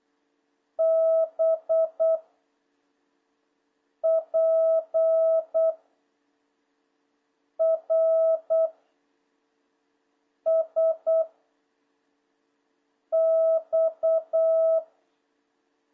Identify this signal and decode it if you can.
Morse code, heard as a keyed tone, about 8 wpm, 650 Hz. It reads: B P R S X
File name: jh8cbh / 2026cw